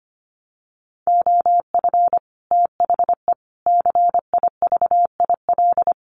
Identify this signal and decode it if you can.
Morse code, heard as a keyed tone, about 25 wpm, 700 Hz. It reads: OF THE CIVIL
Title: Sets of 5 Words Encore2 25wpm